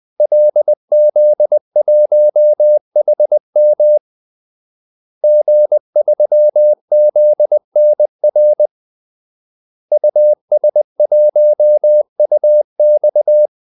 Morse code, heard as a keyed tone, about 20 wpm, 600 Hz. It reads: LZ1HM G3ZNR US1UX
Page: Kaldesignaler 20 wpm